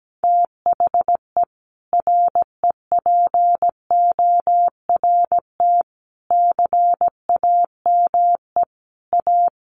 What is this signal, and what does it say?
Morse code, heard as a keyed tone, about 17 wpm, 700 Hz. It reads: THE REPORT CAME A
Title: Sentences from Top 900 Words 17wpm